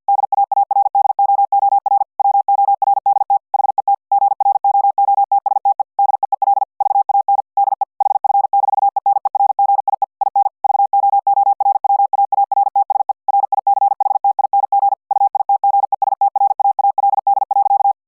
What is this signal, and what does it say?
Morse code, heard as a keyed tone, about 50 wpm, 800 Hz. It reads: DAARDOOR WORDT HET GROOTSTE DEEL VAN DE HF-ENERGIE IN VOORWAARTSE RICHTING UITGESTRAALD.